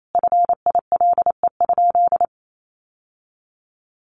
Morse code, heard as a keyed tone, about 28 wpm, 700 Hz. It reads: FILE?
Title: Sentences from Top 1200 Words 28wpm